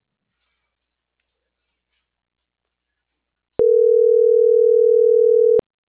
Ringback tone (North America), 440 + 480 Hz.